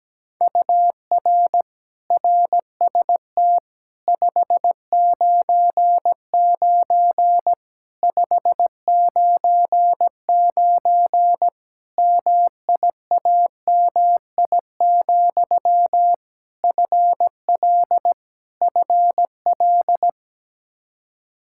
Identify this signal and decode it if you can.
Morse code, heard as a keyed tone, about 17 wpm, 700 Hz. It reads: UR RST 599 599 MIAMI, FL FL
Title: Practice QSO - Exchange 1 - Beginning - 17wpm